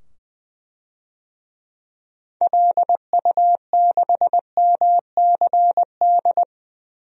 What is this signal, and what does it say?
Morse code, heard as a keyed tone, about 21 wpm, 700 Hz. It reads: LU6MCD
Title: DXCC-100 Call Signs - 6 Characters - 20wpm